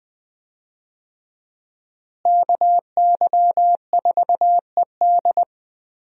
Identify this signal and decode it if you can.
Morse code, heard as a keyed tone, about 20 wpm, 700 Hz. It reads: KY4ED
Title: USA Call Signs - 5 Characters - Repeated 2x - 20wpm